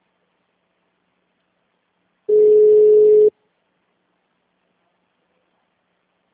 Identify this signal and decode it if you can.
Ringback tone (Europe, 425 Hz).